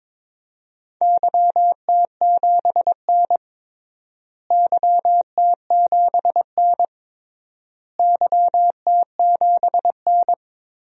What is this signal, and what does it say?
Morse code, heard as a keyed tone, about 22 wpm, 700 Hz. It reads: YT7N YT7N YT7N
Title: DXCC-100 Call Signs - 4 Characters - Repeated 3x - 22wpm